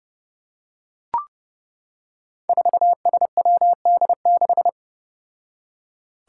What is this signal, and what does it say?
Morse code, heard as a keyed tone, about 30 wpm, 700 Hz. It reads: 4SWD6
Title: Five Challenging Letter-Number Combinations 30wpm